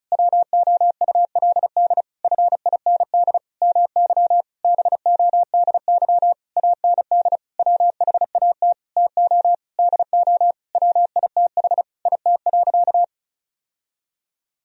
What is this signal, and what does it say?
Morse code, heard as a keyed tone, about 36 wpm, 700 Hz. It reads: WOULD FIND MY BODY AND WHAT TO DO WITH IT.
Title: Sentences from Top 1000 Words - Review Part 2 - 35wpm